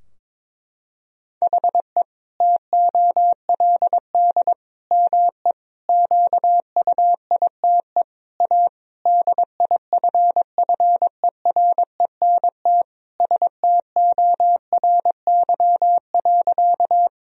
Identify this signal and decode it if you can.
Morse code, heard as a keyed tone, about 21 wpm, 700 Hz. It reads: HE TOLD ME QUITE A DIFFERENT STORY.
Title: Sentences from Top 1000 Words - Review Part 3 - 22wpm